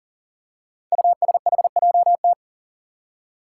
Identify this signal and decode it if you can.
Morse code, heard as a keyed tone, about 42 wpm, 690 Hz. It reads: USHJT
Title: Five Challenging Letter-Number Combinations 40wpm